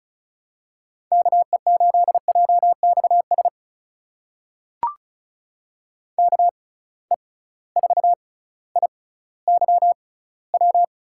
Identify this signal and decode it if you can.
Morse code, heard as a keyed tone, about 36 wpm, spaced as for 18 wpm, 700 Hz. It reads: KE8JXS KE4IYW
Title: USA Call Signs - 6 Characters - 6x Character Spacing - 35wpm